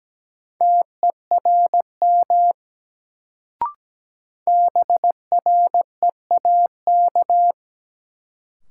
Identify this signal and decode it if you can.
Morse code, heard as a keyed tone, about 17 wpm, 700 Hz. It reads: TERM BREAK
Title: Sentences from Top 900 Words - New Word Drill - 17wpm